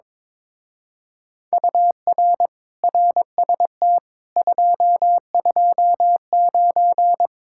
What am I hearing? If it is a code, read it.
Morse code, heard as a keyed tone, about 22 wpm, 700 Hz. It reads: UR RST 229
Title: Speed-Racing Common QSO Phrases - 22wpm